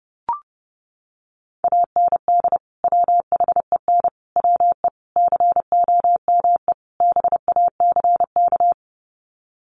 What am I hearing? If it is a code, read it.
Morse code, heard as a keyed tone, about 30 wpm, 700 Hz. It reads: AND WHEN WE COME BACK
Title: Sets of 5 Words 30wpm